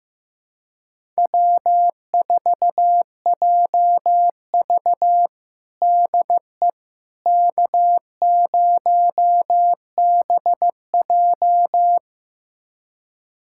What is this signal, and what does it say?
Morse code, heard as a keyed tone, about 15 wpm, 700 Hz. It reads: W4JV DE K0BJ
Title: USA Callsign DE Callsign - 4 Characters - 15wpm